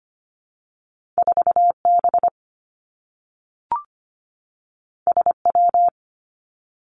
Morse code, heard as a keyed tone, about 25 wpm, 700 Hz. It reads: 4 B S W
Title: Two Challenging Letter-Number Combinations 25wpm